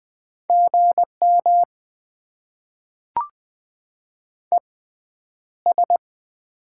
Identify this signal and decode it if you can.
Morse code, heard as a keyed tone, about 20 wpm, spaced as for 10 wpm, 700 Hz. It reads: GM ES
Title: Top 20 QSO Elements - 6x Character Spacing - 20wpm